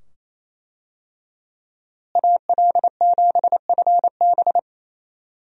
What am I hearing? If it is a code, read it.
Morse code, heard as a keyed tone, about 28 wpm, 700 Hz. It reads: AL7FB
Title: USA Call Signs - 5 Characters - Repeated 2x - 28wpm